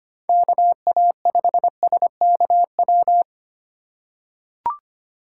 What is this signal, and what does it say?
Morse code, heard as a keyed tone, about 25 wpm, 700 Hz. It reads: KA5SKW
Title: USA Call Signs - 6 Characters - 3x Character Spacing - 25wpm